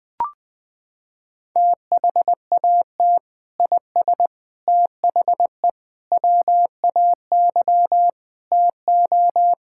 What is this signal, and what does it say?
Morse code, heard as a keyed tone, about 20 wpm, 700 Hz. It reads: THAT IS THE WAY TO
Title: Sentences with 3-Word Phrases - 20wpm